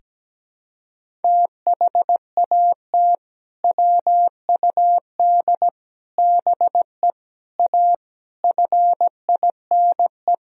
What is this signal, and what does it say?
Morse code, heard as a keyed tone, about 17 wpm, 700 Hz. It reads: THAT WUD BE A FINE
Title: Sentences from Top 500 Words and Abbreviations 17wpm